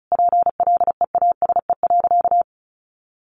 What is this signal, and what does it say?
Morse code, heard as a keyed tone, about 34 wpm, 700 Hz. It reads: PLEASE.
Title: Sentences from Top 1500 Words 35wpm